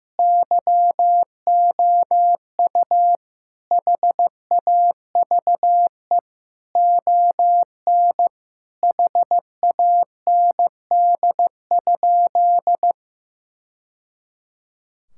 Morse code, heard as a keyed tone, about 15 wpm, 700 Hz. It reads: YOU HAVE ON HAND?
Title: Sentences from Top 1000 Words - Review Part 5 - 15wpm